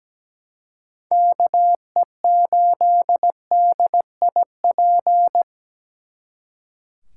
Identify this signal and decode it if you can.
Morse code, heard as a keyed tone, about 20 wpm, 700 Hz. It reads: KE8DIP
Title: Speed Racing - USA 2x3 Call Signs 17wpm